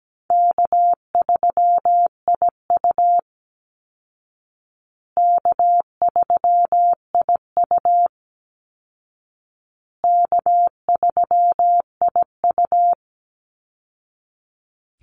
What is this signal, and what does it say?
Morse code, heard as a keyed tone, about 17 wpm, 700 Hz. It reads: K3IU K3IU K3IU
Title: Call Signs Repeated 2x - 17wpm